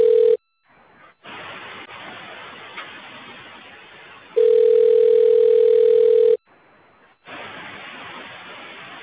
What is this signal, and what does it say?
Ringback tone (North America), 440 + 480 Hz.